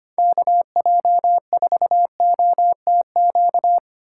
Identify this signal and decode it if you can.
Morse code, heard as a keyed tone, about 25 wpm, 700 Hz. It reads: KJ4OTQ
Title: USA Call Signs - 6 Characters - 3x Character Spacing - 25wpm